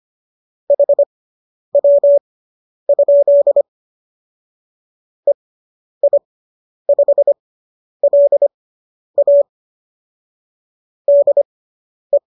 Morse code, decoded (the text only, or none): HW? EI5LA DE